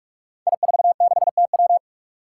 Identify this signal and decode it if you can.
Morse code, heard as a keyed tone, about 44 wpm, 700 Hz. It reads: I46TW
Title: Five Challenging Letter-Number Combinations 45wpm